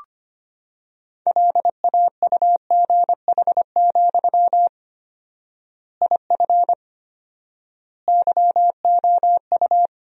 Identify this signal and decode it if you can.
Morse code, heard as a keyed tone, about 25 wpm, 700 Hz. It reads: LAUGH, IF YOU
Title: 4x Word-Spacing - Sentences from Top 500 Words - 25wpm